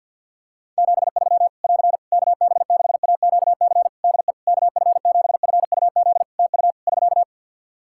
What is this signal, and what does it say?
Morse code, heard as a keyed tone, about 51 wpm, 700 Hz. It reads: 73 <AR> KB6AQC DE KF7FUZ TU <SK>
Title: Practice QSO - Exchange 4 - Ending - 50wpm